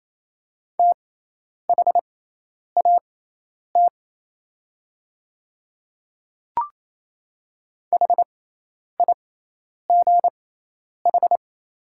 Morse code, heard as a keyed tone, about 27 wpm, spaced as for 10 wpm, 690 Hz. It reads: THAT HIGH
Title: 6x Character Spacing - Top 300 Words - 28wpm